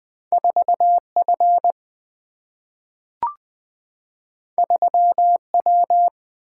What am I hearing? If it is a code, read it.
Morse code, heard as a keyed tone, about 20 wpm, 700 Hz. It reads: 4 F 3 W
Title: Single-Letter Single-Number 20wpm